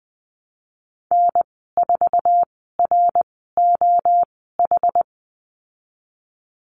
Morse code, heard as a keyed tone, about 20 wpm, spaced as for 14 wpm, 700 Hz. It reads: N4ROH
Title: 088 Lesson 15g - Number 5 - Call Signs 20wpm